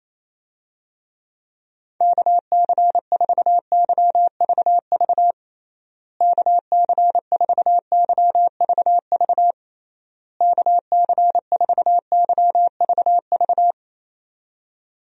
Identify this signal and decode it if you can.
Morse code, heard as a keyed tone, about 28 wpm, 700 Hz. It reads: KC4YVV KC4YVV KC4YVV
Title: USA Call Signs - 6 Characters - Repeated 3x - 28wpm